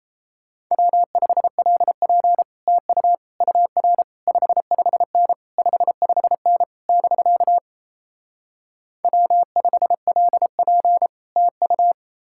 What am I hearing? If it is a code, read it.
Morse code, heard as a keyed tone, about 37 wpm, 700 Hz. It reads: W5LP TU UR 55N 55N <BK> W5LP TU